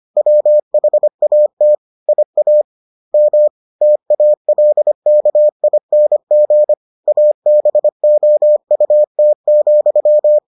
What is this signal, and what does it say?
Morse code, heard as a keyed tone, about 25 wpm, 600 Hz. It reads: WHAT IA M TALKING ABOUT,